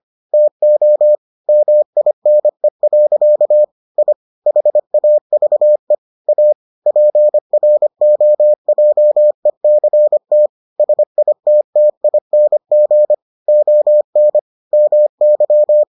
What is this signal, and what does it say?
Morse code, heard as a keyed tone, about 25 wpm, 600 Hz. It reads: TO MINE. I HAVE A PROJECT SITTING ON MY